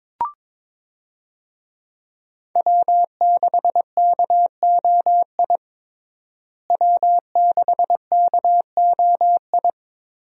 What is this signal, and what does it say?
Morse code, heard as a keyed tone, about 21 wpm, 700 Hz. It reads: W6KOI W6KOI
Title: USA Call Signs - 5 Characters - Repeated 3x - 22wpm